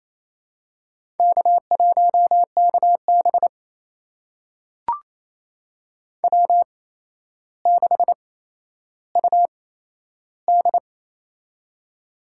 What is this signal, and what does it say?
Morse code, heard as a keyed tone, about 28 wpm, spaced as for 12 wpm, 700 Hz. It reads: K1KB W6UD
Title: USA Call Signs - 4 Characters - 8x Character Spacing - 28wpm